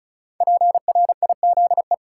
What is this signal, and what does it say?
Morse code, heard as a keyed tone, about 36 wpm, 700 Hz. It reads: PRIZE
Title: Five Letter Words - 35wpm